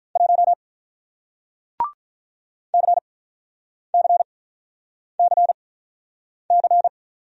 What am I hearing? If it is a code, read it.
Morse code, heard as a keyed tone, about 48 wpm, spaced as for 9 wpm, 700 Hz. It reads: 1 CCCC